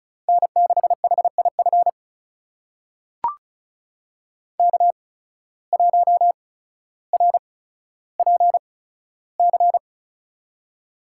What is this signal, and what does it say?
Morse code, heard as a keyed tone, about 36 wpm, spaced as for 15 wpm, 700 Hz. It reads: N6HIF K1RPC
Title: USA Call Signs - 5 Characters - 8x Character Spacing - 35wpm